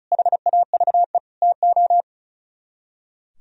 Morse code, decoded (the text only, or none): HAVE TO